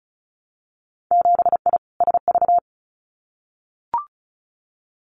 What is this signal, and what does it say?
Morse code, heard as a keyed tone, about 36 wpm, 700 Hz. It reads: 7I SV
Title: Winter Field Day Exchanges Encore 35wpm